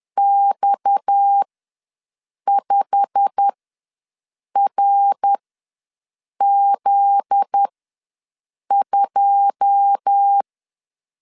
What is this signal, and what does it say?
Morse code, heard as a keyed tone, about 10 wpm, spaced as for 6 wpm, 800 Hz. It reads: X5RZ2
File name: LEZIONI / 9 2 ws 10wpm 4130